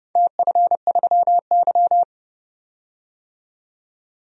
Morse code, heard as a keyed tone, about 30 wpm, 700 Hz. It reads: TF3Y
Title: DXCC-100 Call Signs - 4 Characters - 30wpm